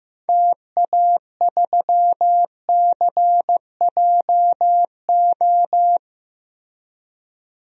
Morse code, heard as a keyed tone, about 15 wpm, 700 Hz. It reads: TA3CJO
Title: DXCC-100 Call Signs - 6 Characters - Repeated 3x - 15wpm